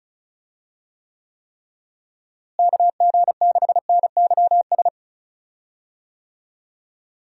Morse code, heard as a keyed tone, about 34 wpm, 700 Hz. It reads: KG6NYS
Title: USA Call Signs - 6 Characters - Repeated 3x - 35wpm